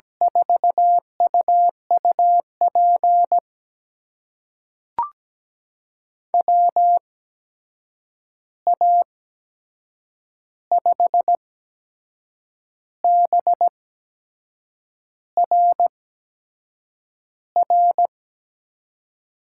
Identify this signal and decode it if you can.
Morse code, heard as a keyed tone, about 17 wpm, spaced as for 6 wpm, 700 Hz. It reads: 4UUP WA5BRR
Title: USA Call Signs - 6 Characters - 8x Character Spacing - 17wpm